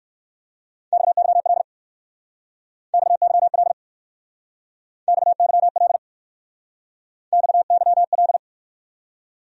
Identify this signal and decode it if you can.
Morse code, heard as a keyed tone, about 56 wpm, 700 Hz. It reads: XYL XYL XYL XYL
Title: Speed-Racing Abbreviations - 45wpm